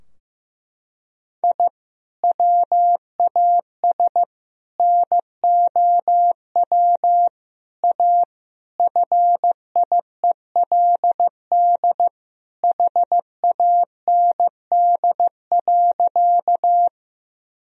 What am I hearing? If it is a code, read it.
Morse code, heard as a keyed tone, about 15 wpm, 700 Hz. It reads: I WAS NOW A FIELD HAND.
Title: Sentences from Top 1000 Words - Review Part 2 - 15wpm